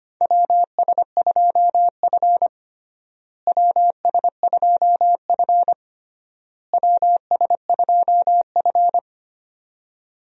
Morse code, heard as a keyed tone, about 25 wpm, 700 Hz. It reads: WS2F WS2F WS2F
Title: USA Call Signs - 4 Characters - Repeated 3x - 25wpm